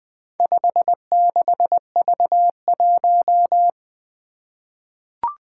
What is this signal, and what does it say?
Morse code, heard as a keyed tone, about 20 wpm, 700 Hz. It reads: 56V1
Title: Four Challenging Letter-Number Combinations 20wpm